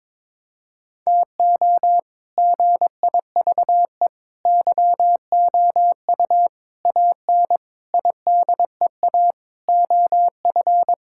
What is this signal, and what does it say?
Morse code, heard as a keyed tone, about 22 wpm, 700 Hz. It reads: TO GIVE YOU AN IDEA OF
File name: Sentences with 7-Word Phrases - 22wpm